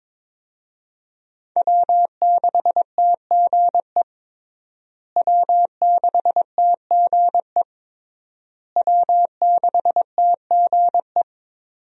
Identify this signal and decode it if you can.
Morse code, heard as a keyed tone, about 22 wpm, 690 Hz. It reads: W6TGE W6TGE W6TGE
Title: USA Call Signs - 5 Characters - Repeated 3x - 22wpm